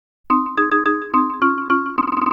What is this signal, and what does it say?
There is no code.